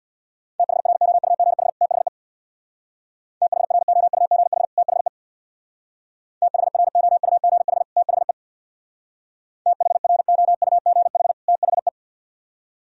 Morse code, heard as a keyed tone, about 62 wpm, 700 Hz. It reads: THROUGH THE THROUGH THE THROUGH THE THROUGH THE
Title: Speed-Racing - Sets of 2 Words 50wpm